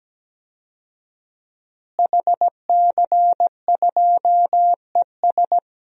Morse code, heard as a keyed tone, about 17 wpm, 700 Hz. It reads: HC2ES
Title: DXCC-100 Call Signs - 5 Characters - Repeated 3x - 17wpm